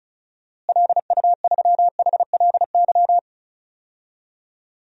Morse code, heard as a keyed tone, about 34 wpm, 700 Hz. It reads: LU3HLY
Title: DXCC-100 Call Signs - 6 Characters - Repeated 3x - 35wpm